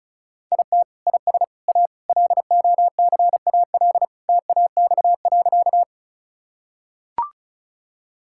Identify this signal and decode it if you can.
Morse code, heard as a keyed tone, about 36 wpm, 700 Hz. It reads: IT IS A LOCAL TAX.
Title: Sentences from Top 1000 Words - Review Part 5 - 35wpm